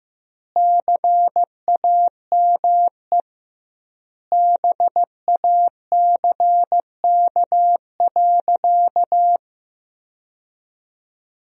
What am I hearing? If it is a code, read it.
Morse code, heard as a keyed tone, about 15 wpm, 700 Hz. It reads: CAME BACK.
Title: 2x Word-Spacing - Sentences from Top 300 Words - 15wpm